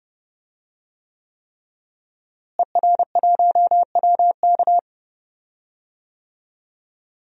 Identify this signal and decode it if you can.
Morse code, heard as a keyed tone, about 32 wpm, 700 Hz. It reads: ER1WK
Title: DXCC-100 Call Signs - 5 Characters - Repeated 3x - 30wpm